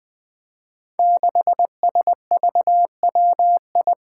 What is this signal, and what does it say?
Morse code, heard as a keyed tone, about 20 wpm, 700 Hz. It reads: 6SVWI
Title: Five Challenging Letter-Number Combinations 20wpm